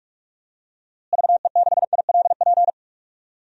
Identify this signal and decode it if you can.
Morse code, heard as a keyed tone, about 47 wpm, 700 Hz. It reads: VE6ILP
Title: DXCC-100 Call Signs - 6 Characters - Repeated 2x - 45wpm